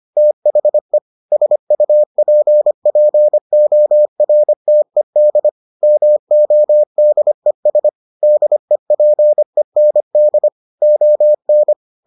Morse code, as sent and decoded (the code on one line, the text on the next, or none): - .... / ... ..- .--. .--. --- .-. - . -.. / -- --- -.. ... / -.. .--. -. -.. / --- -.
THE SUPPORTED MODES DEPEND ON